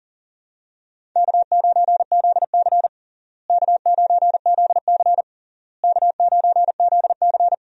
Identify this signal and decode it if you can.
Morse code, heard as a keyed tone, about 40 wpm, 700 Hz. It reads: K9ZC K9ZC K9ZC
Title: USA Call Signs - 4 Characters - Repeated 3x - 40wpm